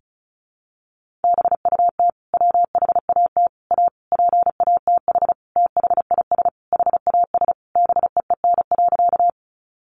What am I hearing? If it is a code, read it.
Morse code, heard as a keyed tone, about 34 wpm, 700 Hz. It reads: BUT WHAT A PATH THIS HAS BEEN.
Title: Sentences from Top 1400 Words 35wpm